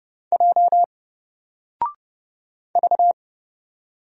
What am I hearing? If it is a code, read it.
Morse code, heard as a keyed tone, about 31 wpm, 700 Hz. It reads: J V
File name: Single-Letter-Number-30wpm